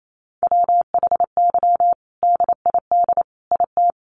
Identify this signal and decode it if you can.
Morse code, heard as a keyed tone, about 28 wpm, 700 Hz. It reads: WHY DID IT
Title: Sentences from Top 400 Words 28wpm